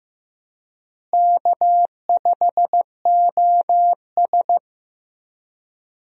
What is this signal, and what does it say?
Morse code, heard as a keyed tone, about 15 wpm, 700 Hz. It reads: K5OS
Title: USA Call Signs - 4 Characters - Repeated 3x - 15wpm